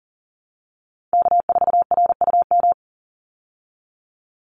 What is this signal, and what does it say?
Morse code, heard as a keyed tone, about 40 wpm, 700 Hz. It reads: K4RUM
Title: Call Signs Repeated 3x - 40wpm